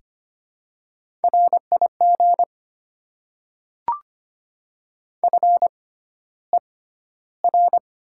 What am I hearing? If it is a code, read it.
Morse code, heard as a keyed tone, about 25 wpm, spaced as for 13 wpm, 690 Hz. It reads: RIG FER
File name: Top 20 QSO Elements - 6x Character Spacing - 25wpm